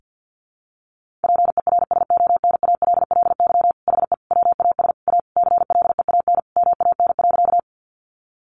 Morse code, heard as a keyed tone, about 49 wpm, 700 Hz. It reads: PERSONALLY HE WAS A CLEAN MAN.
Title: Sentences from Top 1400 Words 50wpm